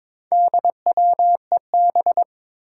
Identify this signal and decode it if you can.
Morse code, heard as a keyed tone, about 22 wpm, 700 Hz. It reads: DWEB
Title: Four Challenging Letter-Number Combinations 22wpm